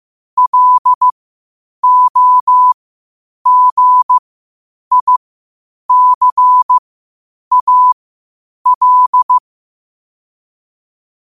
Morse code, heard as a keyed tone, about 15 wpm, spaced as for 8 wpm, 1000 Hz. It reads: LOGICAL
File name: qotdfiles / 8wpm